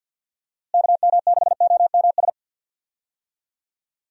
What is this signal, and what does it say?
Morse code, heard as a keyed tone, about 45 wpm, 700 Hz. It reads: KM6OMS
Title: USA Call Signs - 6 Characters - Repeated 2x - 50wpm